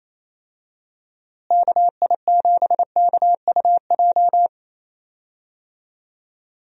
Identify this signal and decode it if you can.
Morse code, heard as a keyed tone, about 27 wpm, 700 Hz. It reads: KI7KUJ
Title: USA Call Signs - 6 Characters - Repeated 3x - 28wpm